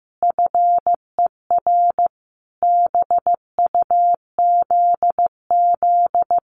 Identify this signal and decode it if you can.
Morse code, heard as a keyed tone, about 15 wpm, 700 Hz. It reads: FER BUZZ
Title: Common QSO Phrases Encore 15wpm